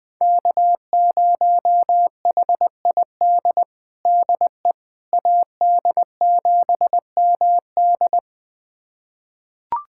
Morse code, heard as a keyed tone, about 20 wpm, 700 Hz. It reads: K0HID DE AD7MD
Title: USA Callsign DE Callsign - 5 Characters - 20wpm